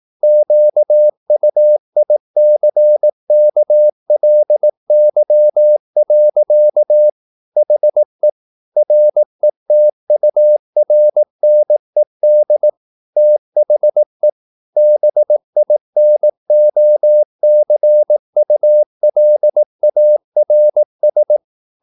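Morse code, heard as a keyed tone, about 18 wpm, 600 Hz. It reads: QUICKLY. HE RETURNED THE BINOCULARS
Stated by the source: Morse practice at 18 wpm, English text, from CW med Gnister.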